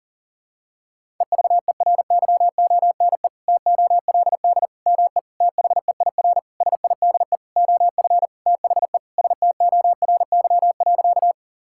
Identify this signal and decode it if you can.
Morse code, heard as a keyed tone, about 40 wpm, 700 Hz. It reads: EVERYONE TOLD ME THEIR SIDE OF THE STORY.
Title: Sentences from Top 1000 Words - Review Part 5 - 40wpm